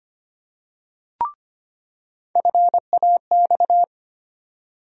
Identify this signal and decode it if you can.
Morse code, heard as a keyed tone, about 25 wpm, 700 Hz. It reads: FAX
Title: USA Call Sign Postfixes - 3 Letters 25wpm